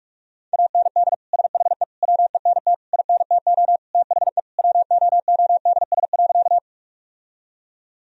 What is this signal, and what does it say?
Morse code, heard as a keyed tone, about 46 wpm, 700 Hz. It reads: AND SHE WENT INTO THE WOODS.
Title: Sentences from Top 1000 Words - Review Part 2 - 45wpm